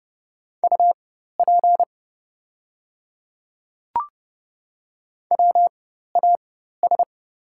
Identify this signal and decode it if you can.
Morse code, heard as a keyed tone, about 31 wpm, spaced as for 14 wpm, 700 Hz. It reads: UP WAS
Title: 4x Character Spacing - Top 100 Words - 30wpm